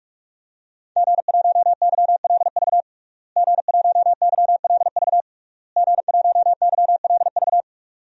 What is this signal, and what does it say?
Morse code, heard as a keyed tone, about 46 wpm, 700 Hz. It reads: G1YLV G1YLV G1YLV
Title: DXCC-100 Call Signs - 5 Characters - Repeated 3x - 45wpm